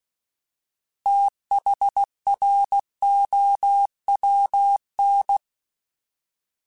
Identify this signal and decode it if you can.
Morse code, heard as a keyed tone, about 16 wpm, 800 Hz. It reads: THROWN
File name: TH / TH mp3s / Fast TH @15